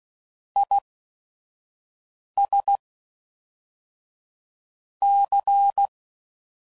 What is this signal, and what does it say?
Morse code, heard as a keyed tone, about 16 wpm, 800 Hz. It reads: I S C